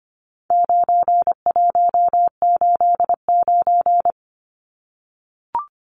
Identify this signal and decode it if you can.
Morse code, heard as a keyed tone, about 25 wpm, 700 Hz. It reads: 9189
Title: Numbers - 4-Digit 25wpm